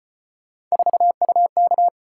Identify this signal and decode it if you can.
Morse code, heard as a keyed tone, about 31 wpm, 700 Hz. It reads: 4UK
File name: Speed-Racing - Three Letter-Number Combinations - 25wpm